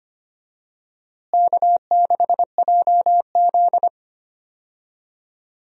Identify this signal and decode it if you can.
Morse code, heard as a keyed tone, about 25 wpm, 700 Hz. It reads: K6JZ
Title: USA Call Signs - 4 Characters - Repeated 2x - 25wpm